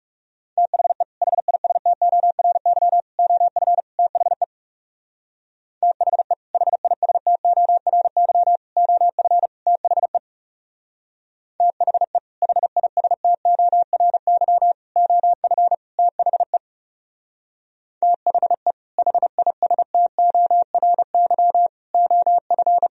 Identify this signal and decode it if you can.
Morse code, heard as a keyed tone, about 38 wpm, 700 Hz. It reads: THE HISTORY OF THE THE HISTORY OF THE THE HISTORY OF THE THE HISTORY OF